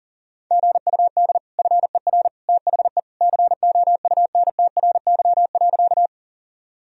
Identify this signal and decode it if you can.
Morse code, heard as a keyed tone, about 40 wpm, 700 Hz. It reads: GUD FER THE COUNTRY.
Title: Sentences from Top 300 Words and Abbreviations 40wpm